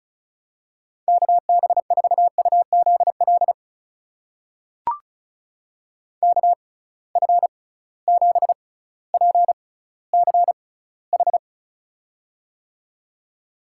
Morse code, heard as a keyed tone, about 34 wpm, spaced as for 18 wpm, 700 Hz. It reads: KB4UZL KF7PCH
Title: USA Call Signs - 6 Characters - 6x Character Spacing - 35wpm